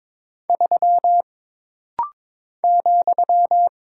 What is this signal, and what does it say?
Morse code, heard as a keyed tone, about 22 wpm, 700 Hz. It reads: 3 ,
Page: This rapid-fire timing is geared to push your ICR (Instant Character Recognition) skills.